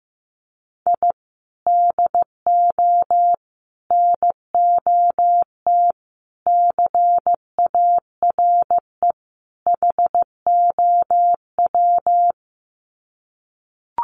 Morse code, heard as a keyed tone, about 15 wpm, 700 Hz. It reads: I DO NOT CARE HOW
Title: Sets of 5 Words Encore2 15wpm